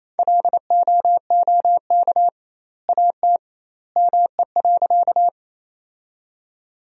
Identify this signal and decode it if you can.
Morse code, heard as a keyed tone, about 28 wpm, 700 Hz. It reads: LOOK AT ME.
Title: 2x Word-Spacing - Sentences from Top 200 Words - 28wpm